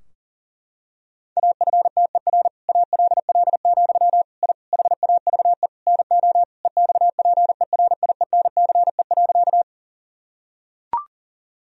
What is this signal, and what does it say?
Morse code, heard as a keyed tone, about 40 wpm, 700 Hz. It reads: AFTER ALL, I HAVE NO EXPERIENCE.
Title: Sentences from Top 1000 Words - Review Part 3 - 40wpm